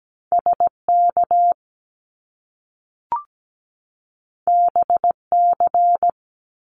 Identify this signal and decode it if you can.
Morse code, heard as a keyed tone, about 17 wpm, 700 Hz. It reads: SK BC
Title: Canadian Provinces and Territories 17wpm